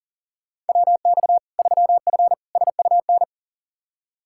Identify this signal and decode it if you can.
Morse code, heard as a keyed tone, about 37 wpm, 700 Hz. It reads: WX 3F SUN
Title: Practice QSO - Exchange 2 - Info - 40wpm